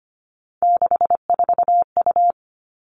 Morse code, heard as a keyed tone, about 25 wpm, 700 Hz. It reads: 64U
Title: Three Challenging Letter-Number Combinations 25wpm